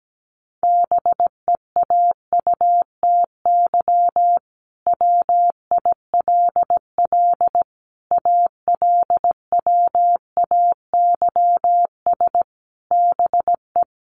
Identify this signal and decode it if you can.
Morse code, heard as a keyed tone, about 17 wpm, 700 Hz. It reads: BEAUTY WILL ALWAYS BE
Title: Sentences from Top 1300 Words 17wpm